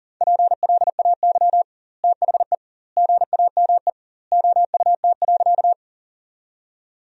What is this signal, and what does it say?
Morse code, heard as a keyed tone, about 40 wpm, 700 Hz. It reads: PLAY THE GAME OUT.
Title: 2x Word-Spacing - Sentences from Top 500 Words - 40wpm